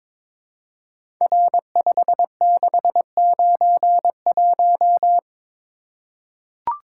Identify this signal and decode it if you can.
Morse code, heard as a keyed tone, about 22 wpm, 700 Hz. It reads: R5691
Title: Code Groups - 5 Letter:Numbers - 22wpm